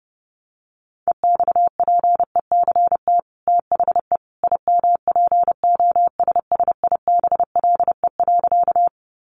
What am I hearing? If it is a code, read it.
Morse code, heard as a keyed tone, about 30 wpm, 700 Hz. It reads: EXPECT THE IMPOSSIBLE.
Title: Sentences from Top 1100 Words 30wpm